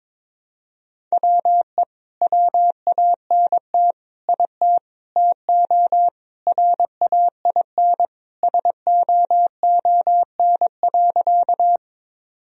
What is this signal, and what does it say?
Morse code, heard as a keyed tone, about 21 wpm, 700 Hz. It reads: WE WANT IT TO RAIN SOON.
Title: Sentences from Top 1000 Words - Review Part 1 - 22wpm